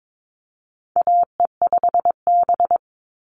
Morse code, heard as a keyed tone, about 22 wpm, 700 Hz. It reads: AE5B
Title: NA SOTA Chaser Callsigns 22wpm